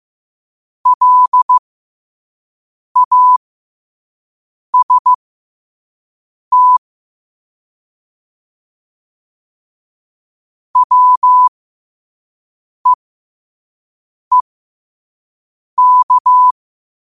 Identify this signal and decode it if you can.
Morse code, heard as a keyed tone, about 15 wpm, spaced as for 5 wpm, 1000 Hz. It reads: LAST WEEK
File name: qotdfiles / 5wpm